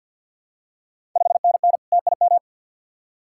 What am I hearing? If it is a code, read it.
Morse code, heard as a keyed tone, about 52 wpm, 690 Hz. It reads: 5NN TIM
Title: Ontario QSO Party Exchange 50wpm